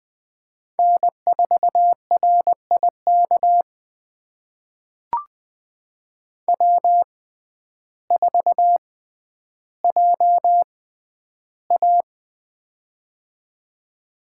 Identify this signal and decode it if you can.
Morse code, heard as a keyed tone, about 20 wpm, spaced as for 11 wpm, 700 Hz. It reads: N4RIK W4JA
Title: USA Call Signs - 6x Character Spacing - 20wpm